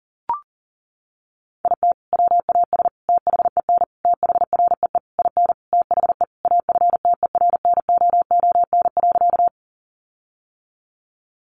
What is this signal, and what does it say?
Morse code, heard as a keyed tone, about 40 wpm, 700 Hz. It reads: IT WAS THEN THREE IN THE AFTERNOON.
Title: Sentences from Top 1500 Words 40wpm